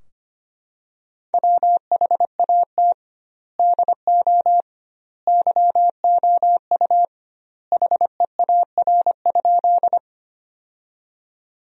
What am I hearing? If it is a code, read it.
Morse code, heard as a keyed tone, about 25 wpm, 700 Hz. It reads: WHAT DO YOU HEAR?
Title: 2x Word-Spacing - Sentences from Top 300 Words - 25wpm